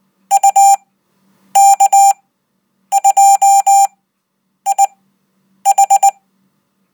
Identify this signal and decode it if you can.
Morse code, heard as a keyed tone, about 20 wpm, spaced as for 9 wpm, 790 Hz. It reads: UK2IH